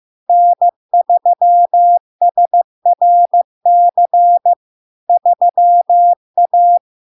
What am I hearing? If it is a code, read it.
Morse code, heard as a keyed tone, about 15 wpm, 700 Hz. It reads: N3SRC 3A
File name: LARK-FD-2022-Log-in-CW-15wpm-700-Hz